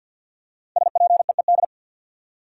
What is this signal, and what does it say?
Morse code, heard as a keyed tone, about 53 wpm, 700 Hz. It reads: SPEED
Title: Five Letter Words - 50wpm